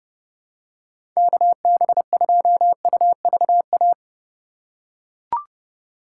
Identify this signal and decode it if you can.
Morse code, heard as a keyed tone, about 32 wpm, 700 Hz. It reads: KB2UVA
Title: USA Call Signs - 6 Characters - 2x Character Spacing - 30wpm